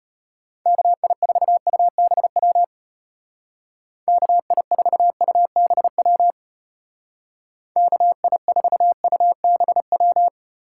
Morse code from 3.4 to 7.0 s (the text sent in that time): KI4UBW